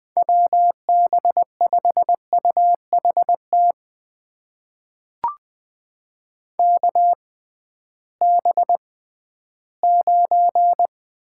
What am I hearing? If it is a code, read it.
Morse code, heard as a keyed tone, about 21 wpm, spaced as for 13 wpm, 700 Hz. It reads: WB5UHT KB9
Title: USA Call Signs - 6 Characters - 6x Character Spacing - 20wpm